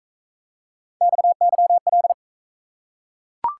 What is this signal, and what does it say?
Morse code, heard as a keyed tone, about 38 wpm, 700 Hz. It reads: XYL